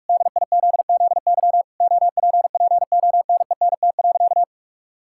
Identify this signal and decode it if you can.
Morse code, heard as a keyed tone, about 46 wpm, 700 Hz. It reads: DIZZY OPPONENT.
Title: Pangram Challenge v2 - 45wpm